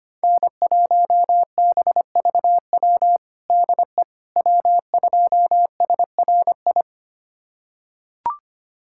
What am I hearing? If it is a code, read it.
Morse code, heard as a keyed tone, about 25 wpm, 700 Hz. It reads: N1BVW DE W2SRI
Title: USA Callsign DE Callsign - 5 Characters - 25wpm